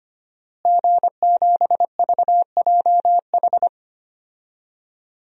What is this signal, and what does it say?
Morse code, heard as a keyed tone, about 25 wpm, 700 Hz. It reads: G7VJH
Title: DXCC-100 Call Signs - 5 Characters - Repeated 3x - 25wpm